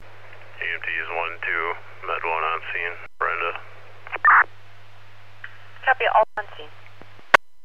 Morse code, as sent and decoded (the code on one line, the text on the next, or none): none